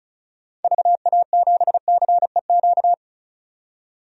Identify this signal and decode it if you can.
Morse code, heard as a keyed tone, about 36 wpm, 700 Hz. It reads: VA7CEQ
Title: DXCC-100 Call Signs - 6 Characters - 35wpm